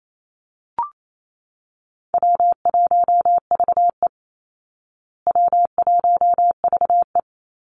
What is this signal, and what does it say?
Morse code, heard as a keyed tone, about 28 wpm, 700 Hz. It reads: W1VE W1VE
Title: Easy Call Signs Repeated 2x - 28wpm